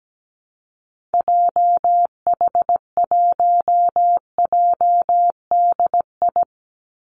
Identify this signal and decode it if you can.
Morse code, heard as a keyed tone, about 17 wpm, 700 Hz. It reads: JH1JDI
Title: Call Signs Repeated 2x - 17wpm